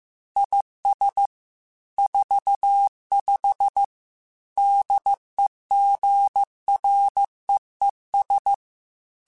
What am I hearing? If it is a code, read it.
Morse code, heard as a keyed tone, about 15 wpm, 800 Hz. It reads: IS 45 DEGREES
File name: Weather Sets / Weather Sets-mp3s / Weather 3 17 @13